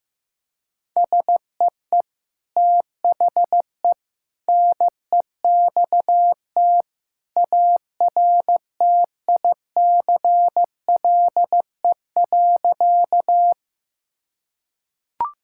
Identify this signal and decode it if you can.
Morse code, heard as a keyed tone, about 15 wpm, 700 Hz. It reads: SEE THE NEXT ARTICLE.
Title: Sentences from Top 1000 Words - Review Part 5 - 15wpm